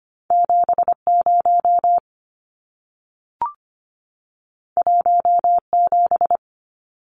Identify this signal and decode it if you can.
Morse code, heard as a keyed tone, about 25 wpm, 700 Hz. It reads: 70 17
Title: Numbers - 2-Digit 25wpm